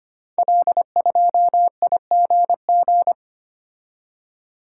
Morse code, heard as a keyed tone, about 25 wpm, 700 Hz. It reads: L2IGG
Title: Code Groups - 5 Letter:Numbers - 25wpm